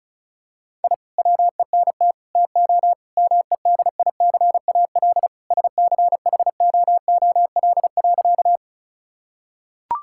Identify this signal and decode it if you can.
Morse code, heard as a keyed tone, about 36 wpm, 700 Hz. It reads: I WENT TO MEDICAL SCHOOL.
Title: Sentences from Top 1000 Words - Review Part 5 - 35wpm